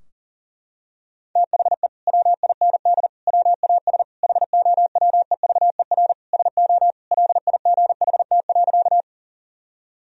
Morse code, heard as a keyed tone, about 40 wpm, 700 Hz. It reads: THE WIND WAS HOWEVER SO LIGHT.
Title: Sentences from Top 1000 Words - Review Part 2 - 40wpm